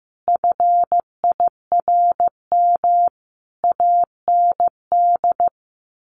Morse code, heard as a keyed tone, about 15 wpm, 700 Hz. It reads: FIRM AND
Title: Sentences from Top 1500 Words 15wpm